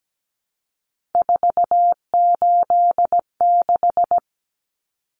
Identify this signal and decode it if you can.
Morse code, heard as a keyed tone, about 17 wpm, 700 Hz. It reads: 486
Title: Common Cut Numbers - 3-Digit 17wpm